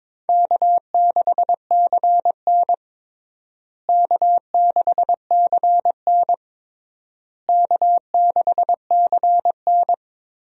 Morse code, heard as a keyed tone, about 22 wpm, 700 Hz. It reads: K6CN K6CN K6CN
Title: USA Call Signs - 4 Characters - Repeated 3x - 22wpm